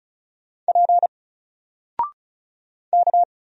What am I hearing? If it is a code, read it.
Morse code, heard as a keyed tone, about 36 wpm, 700 Hz. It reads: P K
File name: Single-Letter-Number-35wpm